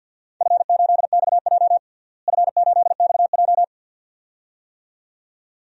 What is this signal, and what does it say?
Morse code, heard as a keyed tone, about 52 wpm, 700 Hz. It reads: F8XJ F8XJ
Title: DXCC-100 Call Signs - 4 Characters - Repeated 2x - 50wpm